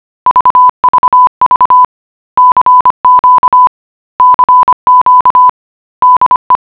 Morse code, heard as a keyed tone, about 25 wpm, 1000 Hz. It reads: VVV CQ CQ DE